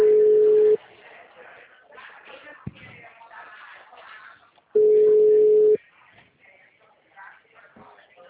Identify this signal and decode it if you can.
Ringback tone (Europe, 425 Hz), with speech.